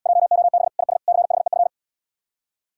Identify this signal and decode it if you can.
Morse code, heard as a keyed tone, about 77 wpm, 700 Hz. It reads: JOB IS CHEF